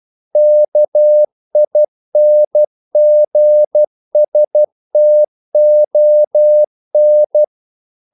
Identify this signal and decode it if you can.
Morse code, heard as a keyed tone, about 12 wpm, 600 Hz. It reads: KINGSTON
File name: CW / Practice / NationalCapitals12